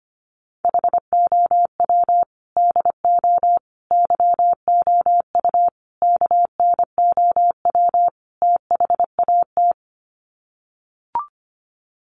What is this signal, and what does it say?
Morse code, heard as a keyed tone, about 25 wpm, 700 Hz. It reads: HOW DO YOU KNOW THAT
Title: Sets of 5 Words Encore 25wpm